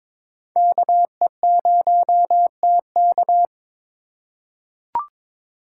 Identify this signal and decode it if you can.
Morse code, heard as a keyed tone, about 22 wpm, 700 Hz. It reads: KE0TK
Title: USA Call Signs - 5 Characters - 4x Character Spacing - 22wpm